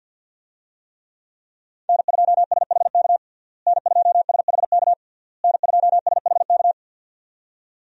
Morse code, heard as a keyed tone, about 52 wpm, 700 Hz. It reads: N2SHK N2SHK N2SHK
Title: USA Call Signs - 5 Characters - Repeated 3x - 50wpm